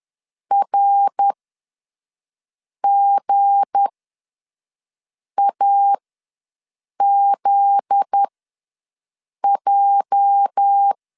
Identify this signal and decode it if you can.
Morse code, heard as a keyed tone, about 10 wpm, spaced as for 5 wpm, 800 Hz. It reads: RGAZJ
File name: LEZIONI / 8 YJ45 / 8 YJ ws 10wpm 4043